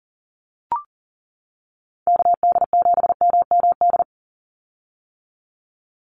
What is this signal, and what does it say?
Morse code, heard as a keyed tone, about 40 wpm, 700 Hz. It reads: KD7MMD
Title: USA Call Signs 40wpm